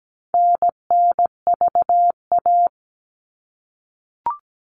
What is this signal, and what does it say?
Morse code, heard as a keyed tone, about 17 wpm, 700 Hz. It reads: NNVA
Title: All Cut Numbers - 4-Digit 17wpm